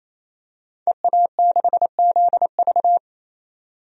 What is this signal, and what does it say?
Morse code, heard as a keyed tone, about 28 wpm, 700 Hz. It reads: EA6ZV
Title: DXCC-100 Call Signs - 5 Characters - 28wpm